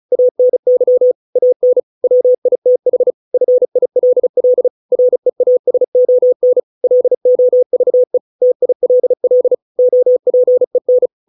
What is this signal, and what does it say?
Morse code, heard as a keyed tone, about 34 wpm, 500 Hz. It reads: ANY AN WITH FILL REASON LOVE TILL OPEN